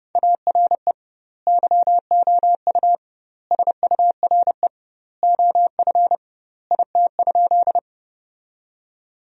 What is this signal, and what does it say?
Morse code, heard as a keyed tone, about 32 wpm, 700 Hz. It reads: ARE YOU SURE OF IT?
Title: 2x Word-Spacing - Sentences from Top 300 Words - 30wpm